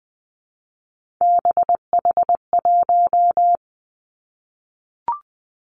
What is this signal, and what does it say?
Morse code, heard as a keyed tone, about 20 wpm, 700 Hz. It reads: BH1
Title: Three Challenging Letter-Number Combinations 20wpm